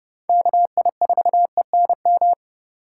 Morse code, heard as a keyed tone, about 31 wpm, 700 Hz. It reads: KI4ENM
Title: USA Call Signs - 6 Characters - 8x Character Spacing - 30wpm